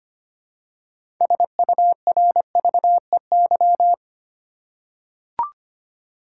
Morse code, heard as a keyed tone, about 25 wpm, 700 Hz. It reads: SURVEY
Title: Six Letter Words - 25wpm